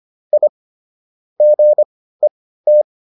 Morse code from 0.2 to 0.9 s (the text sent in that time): I